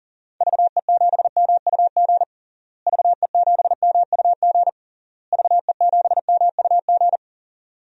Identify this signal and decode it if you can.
Morse code, heard as a keyed tone, about 40 wpm, 700 Hz. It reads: VE7MUG VE7MUG VE7MUG
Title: DXCC-100 Call Signs - 6 Characters - Repeated 3x - 40wpm